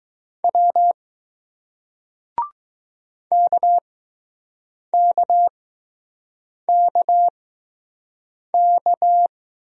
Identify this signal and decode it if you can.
Morse code, heard as a keyed tone, about 20 wpm, spaced as for 6 wpm, 700 Hz. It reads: W KKKK